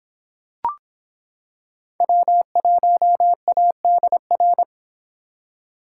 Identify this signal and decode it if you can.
Morse code, heard as a keyed tone, about 25 wpm, 700 Hz. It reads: W1ADR